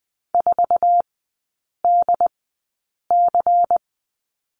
Morse code, heard as a keyed tone, about 20 wpm, spaced as for 8 wpm, 700 Hz. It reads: 4DC
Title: 100 Lesson 17e - Letter u - Three Characters 20wpm